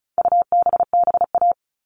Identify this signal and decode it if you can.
Morse code, heard as a keyed tone, about 34 wpm, 700 Hz. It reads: UBBA
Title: All Cut Numbers - 4-Digit 35wpm